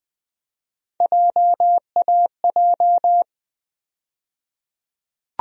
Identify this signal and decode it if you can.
Morse code, heard as a keyed tone, about 20 wpm, 700 Hz. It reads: JAJ
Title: USA Call Sign Postfixes - 3 Letters 20wpm